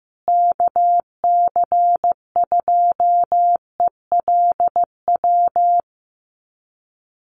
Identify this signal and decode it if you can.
Morse code, heard as a keyed tone, about 15 wpm, 700 Hz. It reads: KC2ELW
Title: USA Call Signs Repeated 2x 15wpm